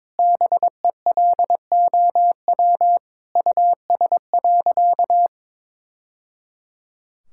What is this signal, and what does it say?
Morse code, heard as a keyed tone, about 22 wpm, 700 Hz. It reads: BELOW US.
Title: Sentences from Top 1000 Words - Review Part 5 - 22wpm